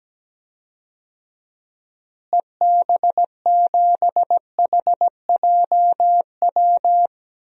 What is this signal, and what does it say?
Morse code, heard as a keyed tone, about 17 wpm, 700 Hz. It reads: EB7HJW
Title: DXCC-100 Call Signs - 6 Characters - Repeated 2x - 17wpm